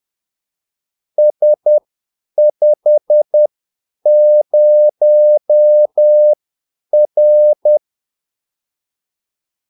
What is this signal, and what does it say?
Morse code, heard as a keyed tone, about 10 wpm, spaced as for 8 wpm, 600 Hz. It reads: S50R
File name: Callsigns-10-8wpm